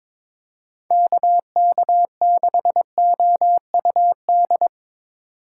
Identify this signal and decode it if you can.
Morse code, heard as a keyed tone, about 22 wpm, 700 Hz. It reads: KK6OUD
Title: USA Call Signs - 6 Characters - 8x Character Spacing - 22wpm